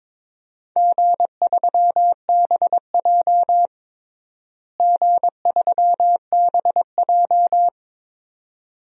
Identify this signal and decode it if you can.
Morse code, heard as a keyed tone, about 22 wpm, 700 Hz. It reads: G3BJ G3BJ
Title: DXCC-100 Call Signs - 4 Characters - Repeated 2x - 22wpm